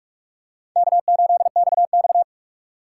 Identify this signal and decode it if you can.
Morse code, heard as a keyed tone, about 48 wpm, 700 Hz. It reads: K8XX
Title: USA Call Signs - 4 Characters - 3x Character Spacing - 45wpm